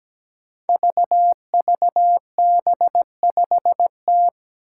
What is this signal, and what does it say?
Morse code, heard as a keyed tone, about 17 wpm, 700 Hz. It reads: VVB5T
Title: Five Challenging Letter-Number Combinations 17wpm